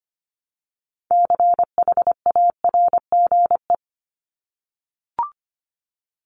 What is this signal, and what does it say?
Morse code, heard as a keyed tone, about 25 wpm, 700 Hz. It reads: CHARGE
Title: Top 751-1000 Words 25wpm